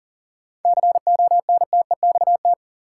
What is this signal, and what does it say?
Morse code, heard as a keyed tone, about 40 wpm, 700 Hz. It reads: CONTEXT
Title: Sentences from Top 1000 Words - Review Part 7 - New Word Drill - 40wpm